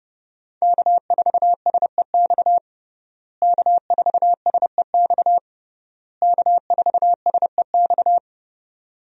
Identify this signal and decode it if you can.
Morse code, heard as a keyed tone, about 30 wpm, 700 Hz. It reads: K4SEX K4SEX K4SEX
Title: USA Call Signs - 5 Characters - Repeated 3x - 30wpm